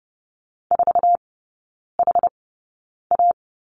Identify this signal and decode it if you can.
Morse code, heard as a keyed tone, about 30 wpm, spaced as for 9 wpm, 700 Hz. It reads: 4HA
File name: Lesson 18e - Letter m - Three Characters 30wpm